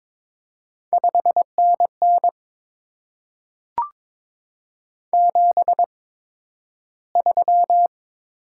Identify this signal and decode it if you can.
Morse code, heard as a keyed tone, about 22 wpm, spaced as for 12 wpm, 700 Hz. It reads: 5NN 73
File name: Top 20 QSO Elements - 8x Character Spacing - 22wpm